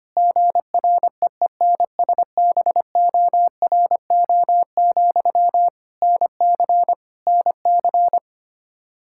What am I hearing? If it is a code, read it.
Morse code, heard as a keyed tone, about 25 wpm, 700 Hz. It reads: GREENSBORO, NC NC
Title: Practice QSO - Exchange 1 - Beginning - 25wpm